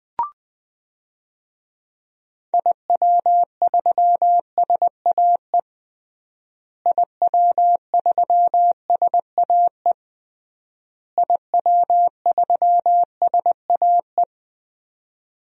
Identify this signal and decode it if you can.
Morse code, heard as a keyed tone, about 21 wpm, 700 Hz. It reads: IW3SAE IW3SAE IW3SAE
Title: DXCC-100 Call Signs - 6 Characters - Repeated 3x - 20wpm